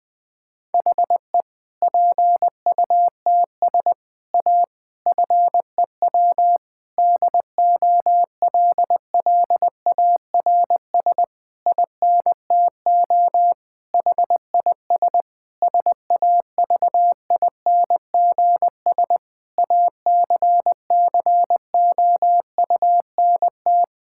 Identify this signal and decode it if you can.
Morse code, heard as a keyed tone, about 20 wpm, 700 Hz. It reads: HE PUTS A FEW DOLLARS INTO HIS SAVINGS ACCOUNT
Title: Difficult Sentences with Hints - 20wpm